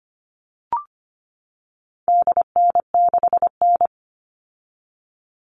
Morse code, heard as a keyed tone, about 25 wpm, 700 Hz. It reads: DN6N
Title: All Cut Numbers - 4-Digit 25wpm